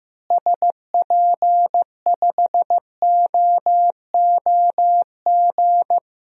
Morse code, heard as a keyed tone, about 15 wpm, 700 Hz. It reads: SP5OOG
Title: DXCC-100 Call Signs - 6 Characters - Repeated 3x - 15wpm